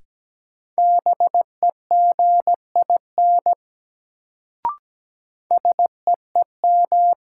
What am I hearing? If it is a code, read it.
Morse code, heard as a keyed tone, about 17 wpm, 700 Hz. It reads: BEGIN SEEM
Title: Sentences from Top 300 Words - New Word Drill - 17wpm